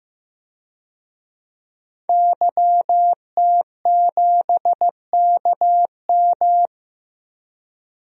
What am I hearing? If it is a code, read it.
Morse code, heard as a keyed tone, about 15 wpm, 700 Hz. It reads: YT7KM
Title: DXCC-100 Call Signs - 5 Characters - Repeated 2x - 15wpm